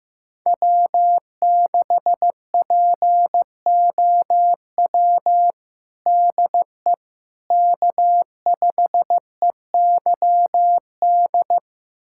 Morse code, heard as a keyed tone, about 15 wpm, 700 Hz. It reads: W6POW DE K5EYD
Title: USA Callsign DE Callsign 15wpm